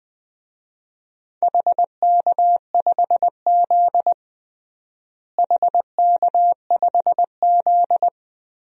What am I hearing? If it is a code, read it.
Morse code, heard as a keyed tone, about 21 wpm, 700 Hz. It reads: HK5Z HK5Z
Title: DXCC-100 Call Signs - 4 Characters - Repeated 2x - 20wpm